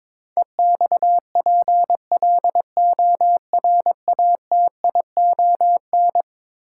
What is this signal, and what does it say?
Morse code, heard as a keyed tone, about 21 wpm, 700 Hz. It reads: EXPLORATION
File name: practice / 22wpm / Eleven Letter Words - 22wpm